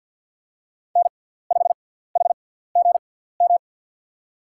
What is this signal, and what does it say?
Morse code, heard as a keyed tone, about 50 wpm, spaced as for 17 wpm, 700 Hz. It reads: N5HGM
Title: USA Call Signs - 6x Character Spacing - 50wpm